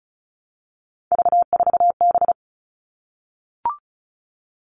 Morse code, heard as a keyed tone, about 34 wpm, 700 Hz. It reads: V4B
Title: All Cut Numbers - 3-Digit 35wpm